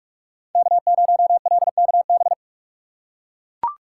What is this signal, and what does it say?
Morse code, heard as a keyed tone, about 43 wpm, 700 Hz. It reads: K0LKB
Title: USA Call Signs - 5 Characters - 2x Character Spacing - 45wpm